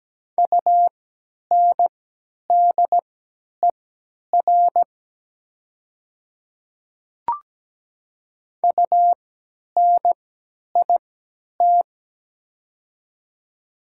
Morse code, heard as a keyed tone, about 17 wpm, spaced as for 10 wpm, 690 Hz. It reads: UNDER UNIT
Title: 3x Character Spacing - Top 500 Words - 17wpm